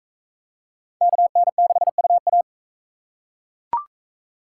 Morse code, heard as a keyed tone, about 38 wpm, 700 Hz. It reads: KN6UA